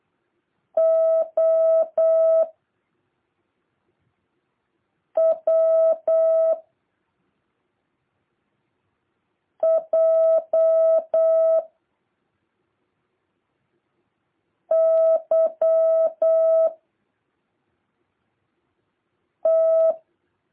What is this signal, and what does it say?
Morse code, heard as a keyed tone, about 8 wpm, 650 Hz. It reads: O W J Y T
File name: jh8cbh / 2023cw